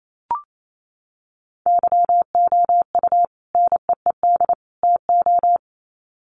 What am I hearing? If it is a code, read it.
Morse code, heard as a keyed tone, about 28 wpm, 700 Hz. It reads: YOU NEED TO
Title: Sets of 3 Words 28wpm